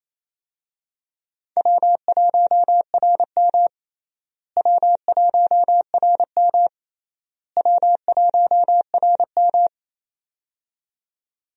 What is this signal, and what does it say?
Morse code, heard as a keyed tone, about 28 wpm, 700 Hz. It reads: W1RM W1RM W1RM
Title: USA Call Signs - 4 Characters - Repeated 3x - 28wpm